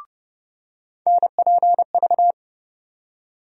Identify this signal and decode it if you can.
Morse code, heard as a keyed tone, about 30 wpm, 700 Hz. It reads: NPV
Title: USA Call Sign Postfixes - 3 Letters 30wpm